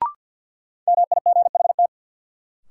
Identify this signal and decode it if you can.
Morse code, heard as a keyed tone, about 53 wpm, 700 Hz. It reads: MIGHT
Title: Sets of 2 Words - New Word Drill - 50wpm